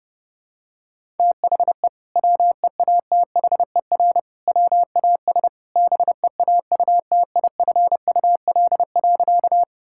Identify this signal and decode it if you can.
Morse code, heard as a keyed tone, about 30 wpm, 700 Hz. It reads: THE WEATHER WAS BEAUTIFUL.
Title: Sentences from Top 1000 Words - Review Part 5 - 30wpm